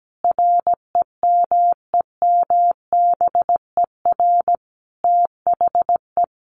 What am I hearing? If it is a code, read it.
Morse code, heard as a keyed tone, about 17 wpm, 700 Hz. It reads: REMEMBER THE
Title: Sentences from Top 800 Words 17wpm